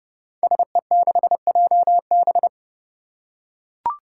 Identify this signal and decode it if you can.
Morse code, heard as a keyed tone, about 30 wpm, 700 Hz. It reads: SE6JB
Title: Five Challenging Letter-Number Combinations 30wpm